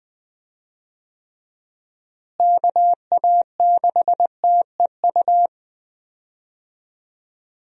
Morse code, heard as a keyed tone, about 19 wpm, 700 Hz. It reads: KA6TEU
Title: USA Call Signs - 6 Characters - Repeated 2x - 20wpm